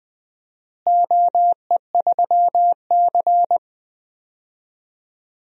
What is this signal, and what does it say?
Morse code, heard as a keyed tone, about 20 wpm, 700 Hz. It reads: OE3C
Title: DXCC-100 Call Signs - 4 Characters - Repeated 2x - 20wpm